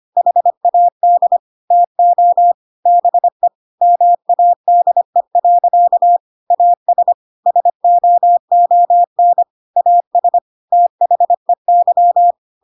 Morse code, heard as a keyed tone, about 25 wpm, 700 Hz. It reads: HAD TO BE MADE. AS SOON AS THEY
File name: Pinocchio / Pinocchio ch3 25 25 ik1hge